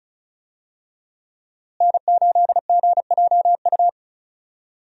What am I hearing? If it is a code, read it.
Morse code, heard as a keyed tone, about 36 wpm, 700 Hz. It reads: N8GJU
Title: USA Call Signs - 5 Characters - Repeated 2x - 35wpm